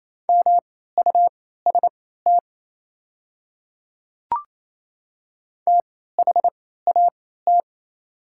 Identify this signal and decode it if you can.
Morse code, heard as a keyed tone, about 27 wpm, spaced as for 16 wpm, 700 Hz. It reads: MUST THAT
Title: 3x Character Spacing - Top 300 Words - 28wpm